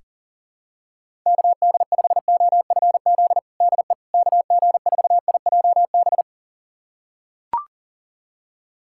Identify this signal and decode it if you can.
Morse code, heard as a keyed tone, about 40 wpm, 700 Hz. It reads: KD5OFZ DE KG4IJB
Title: USA Callsign DE Callsign - 6 Characters - 40wpm